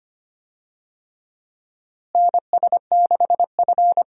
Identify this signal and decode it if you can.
Morse code, heard as a keyed tone, about 25 wpm, 700 Hz. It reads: NS6F
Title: USA Call Signs - 4 Characters - Repeated 3x - 25wpm